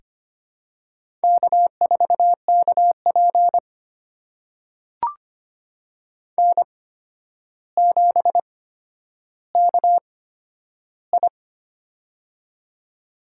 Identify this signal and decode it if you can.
Morse code, heard as a keyed tone, about 25 wpm, spaced as for 11 wpm, 700 Hz. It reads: K4KP N7KI
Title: USA Call Signs - 4 Characters - 8x Character Spacing - 25wpm